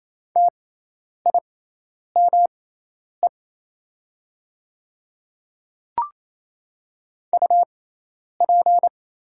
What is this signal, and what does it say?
Morse code, heard as a keyed tone, about 28 wpm, spaced as for 10 wpm, 700 Hz. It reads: TIME UP
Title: 6x Character Spacing - Top 100 Words - 28wpm